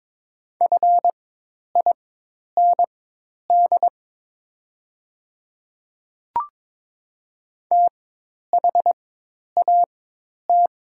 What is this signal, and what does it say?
Morse code, heard as a keyed tone, about 22 wpm, spaced as for 10 wpm, 700 Hz. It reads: FIND THAT
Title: 4x Character Spacing - Top 100 Words - 22wpm